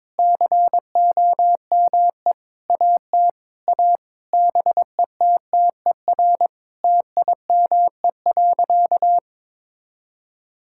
Morse code, heard as a keyed tone, about 22 wpm, 700 Hz. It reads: COME AT A BETTER TIME.
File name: practice / 22wpm / Sentences with 8-Word Phrases - 22wpm